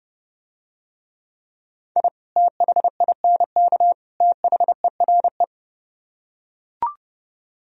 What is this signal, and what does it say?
Morse code, heard as a keyed tone, about 30 wpm, 700 Hz. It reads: I THINK THERE